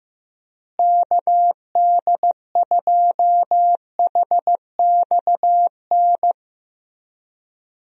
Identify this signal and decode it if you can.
Morse code, heard as a keyed tone, about 15 wpm, 700 Hz. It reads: KD2HXN
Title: USA Call Signs - 6 Characters - Repeated 3x - 15wpm